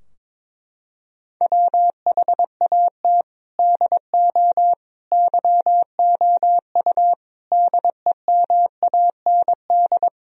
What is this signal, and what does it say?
Morse code, heard as a keyed tone, about 21 wpm, 700 Hz. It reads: WHAT DO YOU DEMAND
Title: Sentences from Top 1000 Words - Review Part 4 - 22wpm